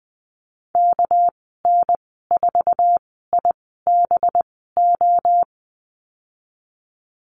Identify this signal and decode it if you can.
Morse code, heard as a keyed tone, about 19 wpm, spaced as for 14 wpm, 700 Hz. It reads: KN4IBO
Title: 207 Lesson 33g - Letter j - Call Signs 20wpm